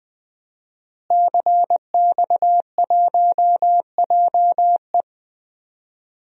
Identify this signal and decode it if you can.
Morse code, heard as a keyed tone, about 20 wpm, 700 Hz. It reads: CX1JE
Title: DXCC-100 Call Signs - 5 Characters - Repeated 3x - 20wpm